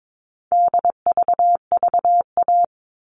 Morse code, heard as a keyed tone, about 22 wpm, 700 Hz. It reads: DVVA
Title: All Cut Numbers - 4-Digit 22wpm